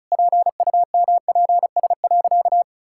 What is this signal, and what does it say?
Morse code, heard as a keyed tone, about 36 wpm, 700 Hz. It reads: PUMPS.
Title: Pangram Challenge v2 - 35wpm